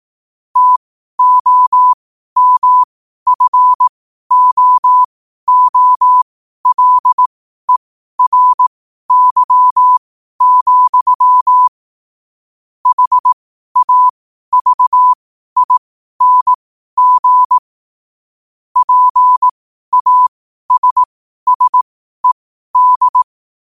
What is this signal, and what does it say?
Morse code, heard as a keyed tone, about 18 wpm, spaced as for 12 wpm, 1000 Hz. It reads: TOMFOOLERY, HAVING PASSED
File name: qotdfiles / 12wpm